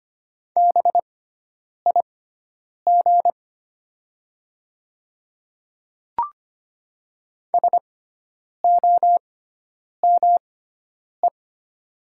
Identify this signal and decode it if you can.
Morse code, heard as a keyed tone, about 25 wpm, spaced as for 9 wpm, 700 Hz. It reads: BIG SOME
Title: 6x Character Spacing - Top 200 Words - 25wpm